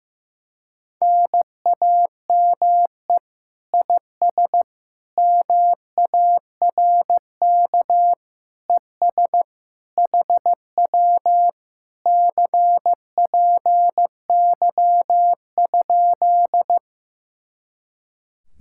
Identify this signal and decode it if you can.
Morse code, heard as a keyed tone, about 15 wpm, 700 Hz. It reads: NAME IS MARK ES HW CPY?
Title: Speed-Racing Common QSO Phrases Encore - 15wpm